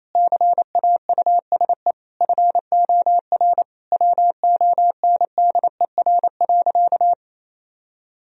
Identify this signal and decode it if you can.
Morse code, heard as a keyed tone, about 28 wpm, 700 Hz. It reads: CAUSE FOR WONDER.
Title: Sentences from Top 1000 Words - Review Part 3 - 28wpm